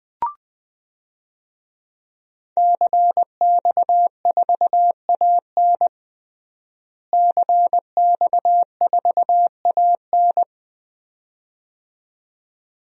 Morse code, heard as a keyed tone, about 21 wpm, 700 Hz. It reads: CX4AN CX4AN
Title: DXCC-100 Call Signs - 5 Characters - Repeated 2x - 20wpm